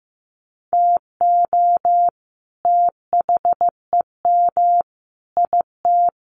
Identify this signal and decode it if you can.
Morse code, heard as a keyed tone, about 15 wpm, 700 Hz. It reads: TO THEM IT
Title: Sentences from Top 1400 Words 15wpm